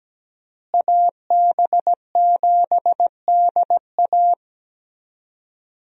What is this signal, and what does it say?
Morse code, heard as a keyed tone, about 17 wpm, 700 Hz. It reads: AB7DA
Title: USA Call Signs - 5 Characters - Repeated 3x - 17wpm